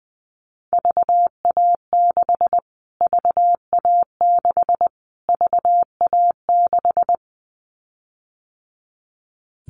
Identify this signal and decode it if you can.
Morse code, heard as a keyed tone, about 20 wpm, 700 Hz. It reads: VA6 VA6 VA6
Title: Callsign Prefixes for Top 25 DXCC - Repeated 3x - 20wpm